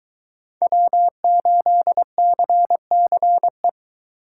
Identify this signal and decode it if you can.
Morse code, heard as a keyed tone, about 22 wpm, 700 Hz. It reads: W8CCE